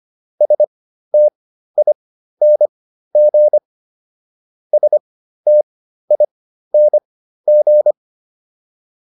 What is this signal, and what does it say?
Morse code, heard as a keyed tone, about 25 wpm, spaced as for 13 wpm, 600 Hz. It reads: STING STING
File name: ING5 13